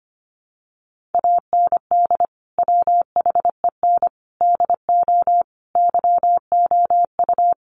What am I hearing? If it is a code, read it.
Morse code, heard as a keyed tone, about 25 wpm, 700 Hz. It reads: AND WHEN DO YOU
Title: Sentences from Top 1100 Words 25wpm